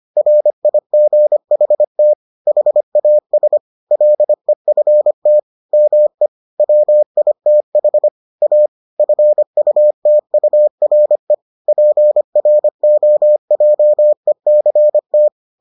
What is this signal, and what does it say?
Morse code, heard as a keyed tone, about 25 wpm, 600 Hz. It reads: RIGHT HAS LEFT ME WITH A FUTURE PROJECT